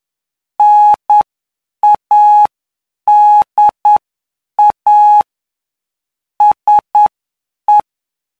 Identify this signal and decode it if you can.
Morse code, heard as a keyed tone, about 9 wpm, spaced as for 7 wpm, 840 Hz. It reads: NADA SE
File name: cw / py2lcd / 17-PY2LCD-Li��o 26